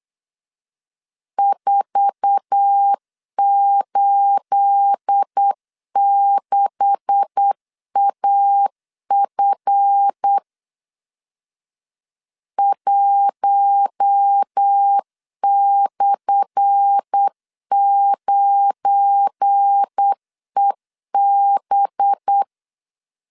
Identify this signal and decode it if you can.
Morse code, heard as a keyed tone, about 8 wpm, 800 Hz. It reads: 486AF 1/9EB
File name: LEZIONI / 10 SIGNS / 10A SLASH 8wpm 3452